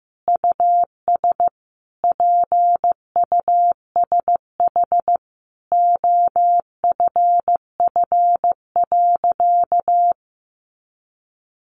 Morse code, heard as a keyed tone, about 15 wpm, 700 Hz. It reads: US PUSH OFF.
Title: Sentences from Top 1300 Words 15wpm